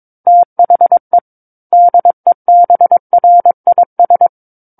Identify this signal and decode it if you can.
Morse code, heard as a keyed tone, about 22 wpm, 700 Hz. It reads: THE DEBRIS
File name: Book One / 13-Chapter 13 - 22 WPM